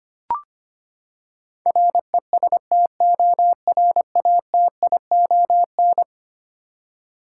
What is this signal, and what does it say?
Morse code, heard as a keyed tone, about 25 wpm, 700 Hz. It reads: RESTORATION
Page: morse-code-ninja-practice-25wpm